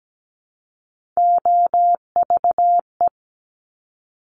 Morse code, heard as a keyed tone, about 17 wpm, 700 Hz. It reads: OVE
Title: Top Three Letter Combinations in Words 17wpm